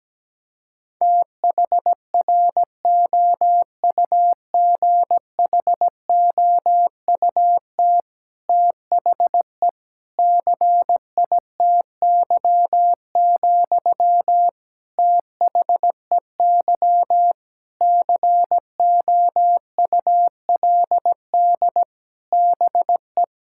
Morse code, heard as a keyed tone, about 17 wpm, 700 Hz. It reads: THROUGHOUT THE CITY, THEY COULD BE
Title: Sentences from Top 1000 Words - Review Part 5 - 17wpm